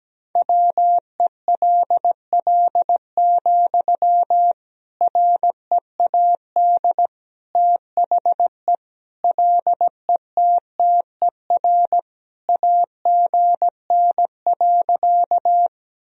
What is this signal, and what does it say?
Morse code, heard as a keyed tone, about 17 wpm, 700 Hz. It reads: WELL, READ THE LETTER AGN.
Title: Sentences from Top 300 Words and Abbreviations 17wpm